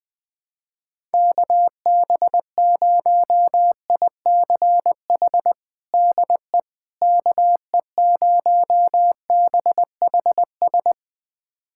Morse code, heard as a keyed tone, about 20 wpm, 700 Hz. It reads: KB0ICH DE KE0BHS
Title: USA Callsign DE Callsign - 6 Characters - 20wpm